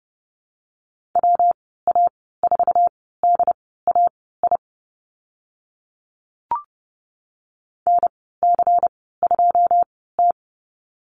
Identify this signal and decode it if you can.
Morse code, heard as a keyed tone, about 30 wpm, spaced as for 17 wpm, 700 Hz. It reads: WA4DAI NC2T
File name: Lesson 19g - Letter w - Call Signs 30wpm